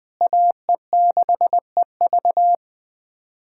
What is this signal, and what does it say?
Morse code, heard as a keyed tone, about 20 wpm, 700 Hz. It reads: AE6EV
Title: USA Callsign DE Callsign - 5 Characters - 20wpm